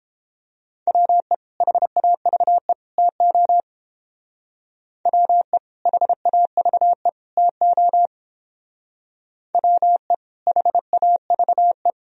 Morse code, heard as a keyed tone, about 28 wpm, 700 Hz. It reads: WE HAVE TO WE HAVE TO WE HAVE